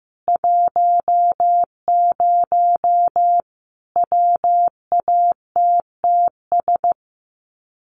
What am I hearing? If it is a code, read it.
Morse code, heard as a keyed tone, about 15 wpm, 700 Hz. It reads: 10 WATTS
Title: Common QSO Phrases 15wpm